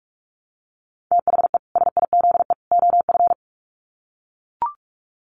Morse code, heard as a keyed tone, about 44 wpm, 700 Hz. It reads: THE SIZE OF
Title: Sets of 3 Words 45wpm